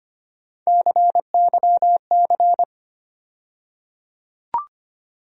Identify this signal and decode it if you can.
Morse code, heard as a keyed tone, about 25 wpm, 700 Hz. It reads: CYC
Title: USA Call Sign Postfixes - 3 Letters 25wpm